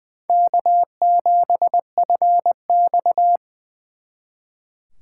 Morse code, heard as a keyed tone, about 24 wpm, 700 Hz. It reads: K7FX
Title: Speed Racing - USA 1x2 Call Signs 20wpm